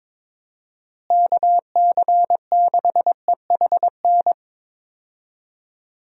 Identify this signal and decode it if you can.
Morse code, heard as a keyed tone, about 22 wpm, 700 Hz. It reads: KC6EHN
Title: USA Call Signs - 6 Characters - Repeated 3x - 22wpm